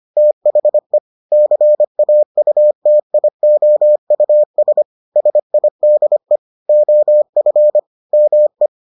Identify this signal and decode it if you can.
Morse code, heard as a keyed tone, about 25 wpm, 600 Hz. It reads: THE CAUTIOUS SIDE OF ME